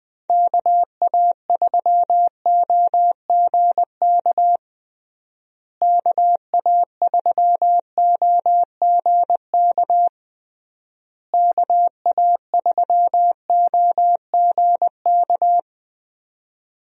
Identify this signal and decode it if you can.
Morse code, heard as a keyed tone, about 20 wpm, 700 Hz. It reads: KA3OGK KA3OGK KA3OGK
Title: USA Call Signs - 6 Characters - Repeated 3x - 20wpm